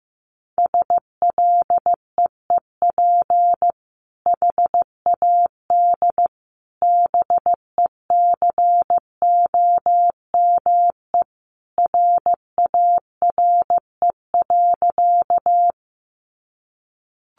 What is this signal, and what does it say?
Morse code, heard as a keyed tone, about 15 wpm, 700 Hz. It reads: SLEEP HAD BECOME RARE.
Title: Sentences from Top 1100 Words 15wpm